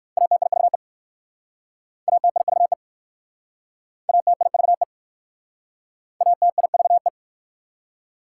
Morse code, heard as a keyed tone, about 60 wpm, 690 Hz. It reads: ATIVE ATIVE ATIVE ATIVE